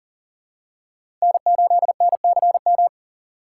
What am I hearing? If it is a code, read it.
Morse code, heard as a keyed tone, about 38 wpm, 700 Hz. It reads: N8NCM
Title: USA Call Signs - 5 Characters - 6x Character Spacing - 40wpm